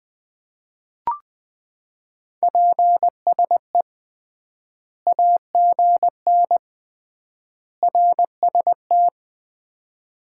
Morse code, heard as a keyed tone, about 21 wpm, 700 Hz. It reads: PSE AGN RST
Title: Common QSO Phrases - 3x Word-Spacing - 20wpm